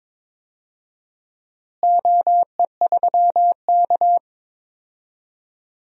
Morse code, heard as a keyed tone, about 21 wpm, 700 Hz. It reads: OE3K
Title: DXCC-100 Call Signs - 4 Characters - Repeated 3x - 22wpm